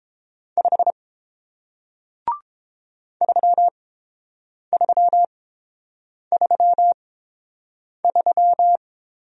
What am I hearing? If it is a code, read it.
Morse code, heard as a keyed tone, about 30 wpm, spaced as for 7 wpm, 700 Hz. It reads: 5 3333